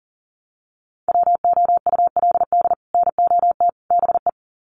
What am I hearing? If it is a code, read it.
Morse code, heard as a keyed tone, about 40 wpm, 700 Hz. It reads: WOULD NOT BE
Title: Sets of 3 Words 40wpm